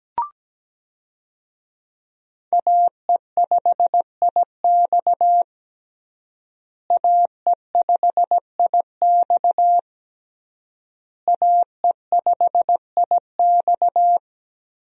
Morse code, heard as a keyed tone, about 17 wpm, 700 Hz. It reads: AE5IX AE5IX AE5IX
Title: USA Call Signs - 5 Characters - Repeated 3x - 17wpm